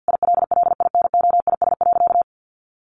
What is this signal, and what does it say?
Morse code, heard as a keyed tone, about 50 wpm, 700 Hz. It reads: ILLINOIS.
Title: A Sound Of Thunder 50wpm